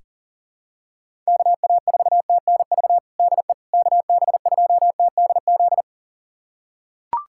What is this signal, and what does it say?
Morse code, heard as a keyed tone, about 40 wpm, 700 Hz. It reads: KA4TNV DE KB2TDZ
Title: USA Callsign DE Callsign - 6 Characters - 40wpm